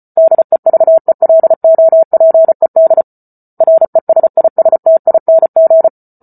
Morse code, heard as a keyed tone, about 34 wpm, 650 Hz. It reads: DEVELOPED RESISTING
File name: Book Two / 08-Chapter 8 - 34 WPM